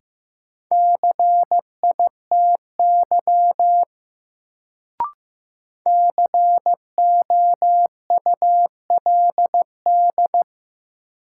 Morse code, heard as a keyed tone, about 15 wpm, 700 Hz. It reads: CITY COULD
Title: Sets of 2 Words - New Word Drill - 15wpm